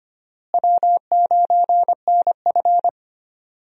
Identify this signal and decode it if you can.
Morse code, heard as a keyed tone, about 25 wpm, 700 Hz. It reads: W9NF
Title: USA Call Signs - 4 Characters - Repeated 2x - 25wpm